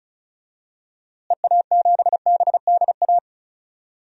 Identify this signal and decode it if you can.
Morse code, heard as a keyed tone, about 36 wpm, 700 Hz. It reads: EA7BDA
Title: DXCC-100 Call Signs - 6 Characters - Repeated 2x - 35wpm